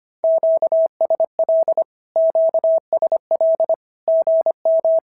Morse code, heard as a keyed tone, about 25 wpm, 650 Hz. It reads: QSL QSL GM
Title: SOTA QSO Practice 25wpm